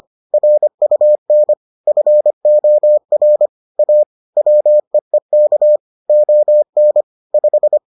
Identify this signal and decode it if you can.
Morse code, heard as a keyed tone, about 25 wpm, 600 Hz. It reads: RUN FOR A WEEK ON 5